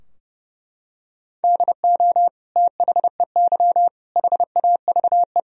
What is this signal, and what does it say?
Morse code, heard as a keyed tone, about 30 wpm, 700 Hz. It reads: DO THEY HAVE
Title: Sentences from Top 1000 Words - Review Part 4 - 30wpm